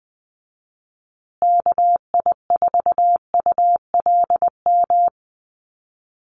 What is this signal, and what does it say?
Morse code, heard as a keyed tone, about 20 wpm, 700 Hz. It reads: KI4ULM
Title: USA Call Signs Repeated 3x 20wpm